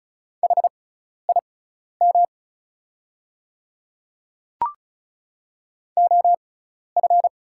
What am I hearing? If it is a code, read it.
Morse code, heard as a keyed tone, about 36 wpm, spaced as for 12 wpm, 700 Hz. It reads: HIM OF
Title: 6x Character Spacing - Top 100 Words - 35wpm